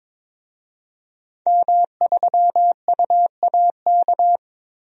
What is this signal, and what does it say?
Morse code, heard as a keyed tone, about 22 wpm, 700 Hz. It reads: M3UAK
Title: DXCC-100 Call Signs - 5 Characters - Repeated 2x - 22wpm